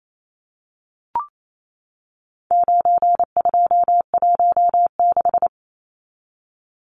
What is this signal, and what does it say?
Morse code, heard as a keyed tone, about 28 wpm, 700 Hz. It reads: 9216
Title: Numbers - 4-Digit 28wpm